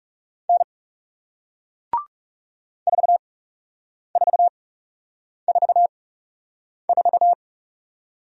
Morse code, heard as a keyed tone, about 40 wpm, spaced as for 8 wpm, 700 Hz. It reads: N 4444